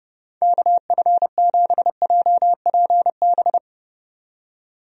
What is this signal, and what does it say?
Morse code, heard as a keyed tone, about 31 wpm, 700 Hz. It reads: KF7JPB
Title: USA Call Signs - 6 Characters - 30wpm